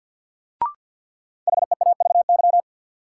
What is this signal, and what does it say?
Morse code, heard as a keyed tone, about 49 wpm, 700 Hz. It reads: HEAVY
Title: Sentences from Top 1000 Words - Review Part 6 - New Word Drill - 50wpm